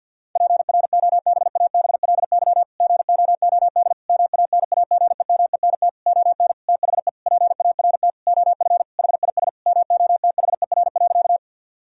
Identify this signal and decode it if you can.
Morse code, heard as a keyed tone, about 50 wpm, 700 Hz. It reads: PROBABLY GOOD MANAGEMENT ON THE PART OF HIS MOTHER.
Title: Sentences from Top 1000 Words - Review Part 4 - 50wpm